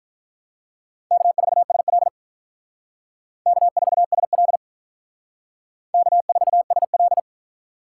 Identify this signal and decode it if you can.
Morse code, heard as a keyed tone, about 44 wpm, 700 Hz. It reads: K4SL K4SL K4SL